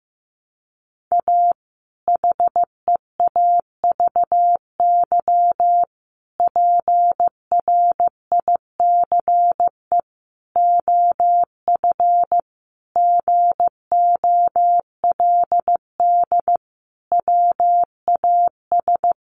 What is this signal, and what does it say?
Morse code, heard as a keyed tone, about 15 wpm, 700 Hz. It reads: A HEAVY PRICE OF GOLD WAS
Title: Sentences from Top 1100 Words 15wpm